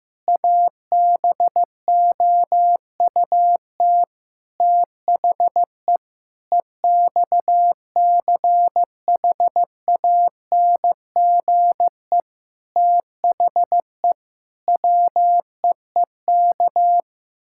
Morse code, heard as a keyed tone, about 15 wpm, 700 Hz. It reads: ABOUT THE EXCHANGE THE WEEK
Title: Sentences from Top 1000 Words - Review Part 6 - 15wpm